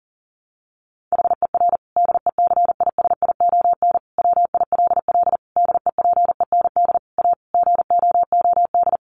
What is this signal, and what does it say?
Morse code, heard as a keyed tone, about 40 wpm, 700 Hz. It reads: HER DECISION WILL DEPEND A GOOD
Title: Sentences from Top 1300 Words 40wpm